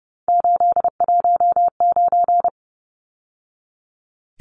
Morse code, heard as a keyed tone, about 30 wpm, 700 Hz. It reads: 819
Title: Numbers - 3-Digit 30wpm